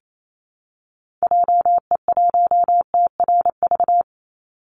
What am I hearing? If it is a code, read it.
Morse code, heard as a keyed tone, about 28 wpm, 700 Hz. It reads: JE1TRV
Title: Call Signs Repeated 3x - 28wpm